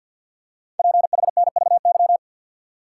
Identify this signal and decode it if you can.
Morse code, heard as a keyed tone, about 52 wpm, 700 Hz. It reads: PHNVY
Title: Code Groups - 5 Letters - 50wpm